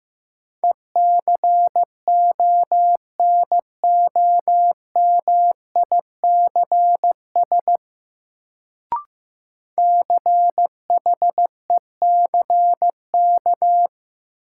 Morse code, heard as a keyed tone, about 15 wpm, 700 Hz. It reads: ECONOMICS CHECK
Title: Sentences from Top 1000 Words - Review Part 3 - New Word Drill - 15wpm